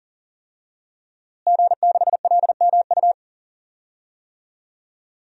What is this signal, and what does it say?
Morse code, heard as a keyed tone, about 40 wpm, 700 Hz. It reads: G6LMU
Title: DXCC-100 Call Signs - 5 Characters - Repeated 3x - 40wpm